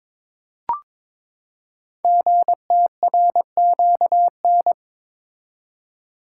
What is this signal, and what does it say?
Morse code, heard as a keyed tone, about 22 wpm, 700 Hz. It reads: GTRQN
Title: Code Groups - 5 Letters - 22wpm